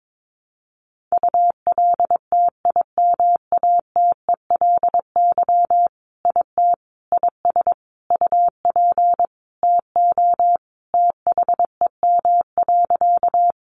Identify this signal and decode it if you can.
Morse code, heard as a keyed tone, about 22 wpm, 700 Hz. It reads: ULTIMATELY IT IS UP TO THEM.
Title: Sentences from Top 1200 Words 22wpm